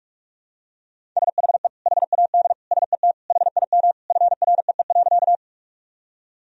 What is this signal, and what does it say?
Morse code, heard as a keyed tone, about 41 wpm, 700 Hz. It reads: SHE HAD SET HIM FREE.
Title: Sentences from Top 1000 Words - Review Part 1 - 45wpm